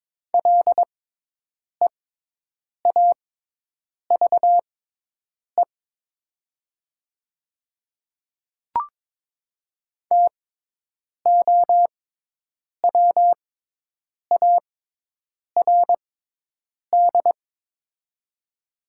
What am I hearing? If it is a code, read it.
Morse code, heard as a keyed tone, about 22 wpm, spaced as for 8 wpm, 700 Hz. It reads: LEAVE TOWARD
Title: 6x Character Spacing - Top 500 Words - 22wpm